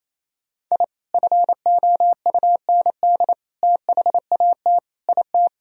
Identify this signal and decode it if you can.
Morse code, heard as a keyed tone, about 28 wpm, 700 Hz. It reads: I FOUND THAT IT
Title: Sentences from Top 1000 Words - Review Part 1 - 28wpm